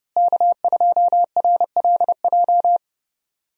Morse code, heard as a keyed tone, about 30 wpm, 700 Hz. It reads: K2RLJ
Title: USA Call Signs - 5 Characters - Repeated 2x - 30wpm